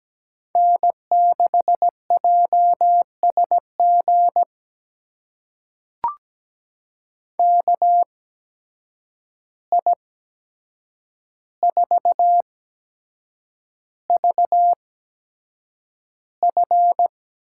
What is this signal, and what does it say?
Morse code, heard as a keyed tone, about 17 wpm, spaced as for 7 wpm, 700 Hz. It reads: N6JSG KI4VF
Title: USA Call Signs - 8x Character Spacing - 17wpm